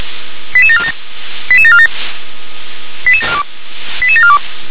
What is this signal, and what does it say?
Five-tone selective call (ZVEI), heard as quick runs of five tones, tones of about 80 ms. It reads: 88747 88747 88742